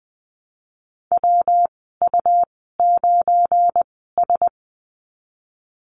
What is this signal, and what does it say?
Morse code, heard as a keyed tone, about 19 wpm, spaced as for 14 wpm, 700 Hz. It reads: WU9S
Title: 179 Lesson 29g - Slash - Call Signs 20wpm